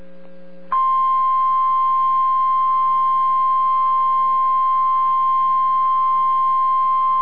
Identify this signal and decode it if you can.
Emergency Alert System (SAME) tones (the 1050 Hz attention tone). It is heard from 1.1 to 6.8 s.